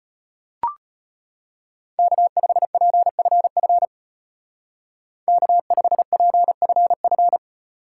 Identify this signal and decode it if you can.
Morse code, heard as a keyed tone, about 35 wpm, 700 Hz. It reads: K5PFF K5PFF